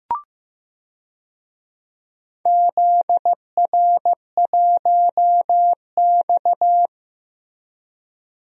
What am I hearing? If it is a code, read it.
Morse code, heard as a keyed tone, about 15 wpm, 700 Hz. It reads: ZR1X
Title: DXCC-100 Call Signs - 4 Characters - Repeated 3x - 15wpm